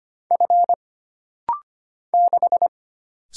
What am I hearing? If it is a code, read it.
Morse code, heard as a keyed tone, about 25 wpm, 700 Hz. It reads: F 6
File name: Single Character (Letter:Number:Punctuation:Prosign) - Rapid-Fire - 25wpm